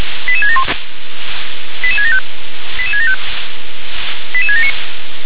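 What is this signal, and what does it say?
Five-tone selective call (ZVEI), heard as quick runs of five tones, tones of about 70 ms. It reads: 88671 88675 88675 88679
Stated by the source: Received on an SDR as a ZVEI five-tone call, code 88670.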